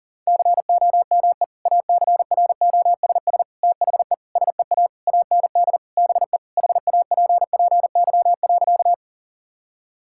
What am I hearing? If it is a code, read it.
Morse code, heard as a keyed tone, about 40 wpm, 700 Hz. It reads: COME ACROSS THE SEA AND BE HAPPY.
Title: Sentences from Top 1000 Words - Review Part 5 - 40wpm